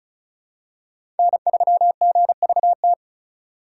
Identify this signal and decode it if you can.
Morse code, heard as a keyed tone, about 34 wpm, 700 Hz. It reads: N3GVT
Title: USA Call Signs - 5 Characters - 3x Character Spacing - 35wpm